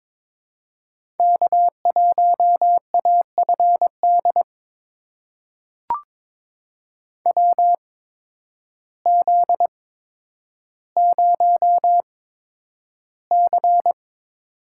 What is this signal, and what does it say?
Morse code, heard as a keyed tone, about 22 wpm, spaced as for 10 wpm, 700 Hz. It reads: K1AFD WZ0C
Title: USA Call Signs - 8x Character Spacing - 22wpm